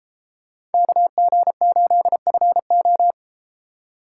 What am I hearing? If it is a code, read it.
Morse code, heard as a keyed tone, about 33 wpm, 700 Hz. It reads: KG8FO